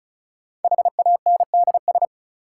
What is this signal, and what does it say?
Morse code, heard as a keyed tone, about 36 wpm, 700 Hz. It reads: HANDS
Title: Sentences from Top 1000 Words - Review Part 6 - New Word Drill - 35wpm